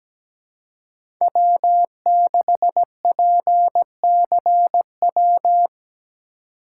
Morse code, heard as a keyed tone, about 17 wpm, 700 Hz. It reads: W6PCW
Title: USA Call Signs - 4x Character Spacing - 17wpm